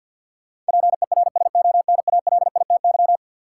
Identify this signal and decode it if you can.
Morse code, heard as a keyed tone, about 53 wpm, 700 Hz. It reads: PERSONALITY
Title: Sentences from Top 1400 Words - New Word Drill - 50wpm